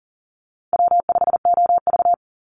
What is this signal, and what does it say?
Morse code, heard as a keyed tone, about 40 wpm, 700 Hz. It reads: W5OV
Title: Easy Call Signs - 40wpm